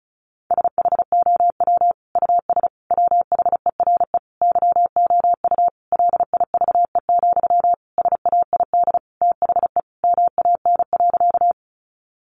Morse code, heard as a keyed tone, about 34 wpm, 700 Hz. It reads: SHOW US WHERE YOU LIVE, SAID THE MAN.
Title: Sentences from Top 200 Words 35wpm